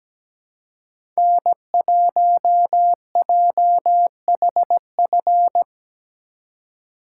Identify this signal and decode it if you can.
Morse code, heard as a keyed tone, about 20 wpm, 700 Hz. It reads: N1JHF
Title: Speed Racing - USA Call Signs 17wpm